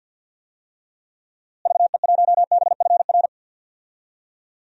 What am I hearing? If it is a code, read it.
Morse code, heard as a keyed tone, about 51 wpm, 700 Hz. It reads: VE1BFR
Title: DXCC-100 Call Signs - 6 Characters - Repeated 2x - 50wpm